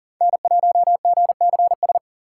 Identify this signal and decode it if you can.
Morse code, heard as a keyed tone, about 40 wpm, 700 Hz. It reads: N1GCS
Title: USA Call Signs - 3x Character Spacing - 40wpm